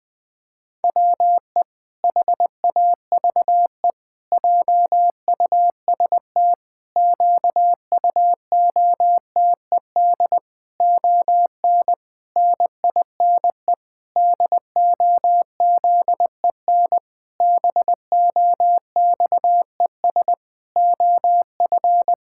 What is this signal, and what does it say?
Morse code, heard as a keyed tone, about 21 wpm, 700 Hz. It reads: WE HAVE JUST QUOTED ON NINE DOZEN BOXES OF
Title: Pangram Challenge v2 - 20wpm